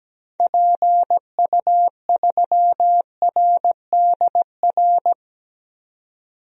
Morse code, heard as a keyed tone, about 17 wpm, 700 Hz. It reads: PU3RDR
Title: DXCC-100 Call Signs - 6 Characters - 17wpm